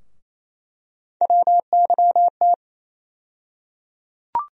Morse code, heard as a keyed tone, about 28 wpm, 700 Hz. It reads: WYT
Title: USA Call Sign Postfixes - 3 Letters 28wpm